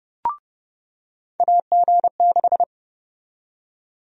Morse code, heard as a keyed tone, about 30 wpm, 700 Hz. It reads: AG6
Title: USA Call Sign Prefixes - 30wpm